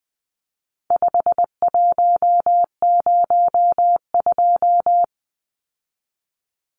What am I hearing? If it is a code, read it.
Morse code, heard as a keyed tone, about 19 wpm, 700 Hz. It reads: 5102
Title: Numbers - 4-Digit 20wpm